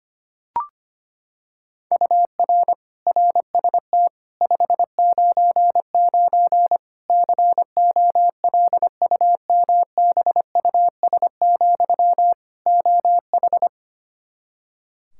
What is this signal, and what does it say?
Morse code, heard as a keyed tone, about 25 wpm, 700 Hz. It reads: UR RST 599 COLUMBUS, OH
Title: K1USN SST QSO Practice 25wpm